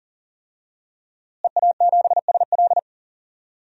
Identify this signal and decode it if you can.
Morse code, heard as a keyed tone, about 40 wpm, 700 Hz. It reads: EA7SL
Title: DXCC-100 Call Signs - 5 Characters - Repeated 3x - 40wpm